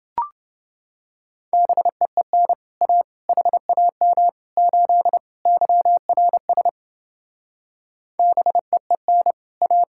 Morse code, heard as a keyed tone, about 30 wpm, 700 Hz. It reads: BEEN A HAM 8 YRS BEEN A